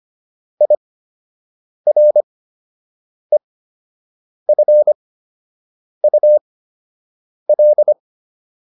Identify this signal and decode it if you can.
Morse code, heard as a keyed tone, about 25 wpm, spaced as for 7 wpm, 600 Hz. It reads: IREFUL
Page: fundamental-practice-files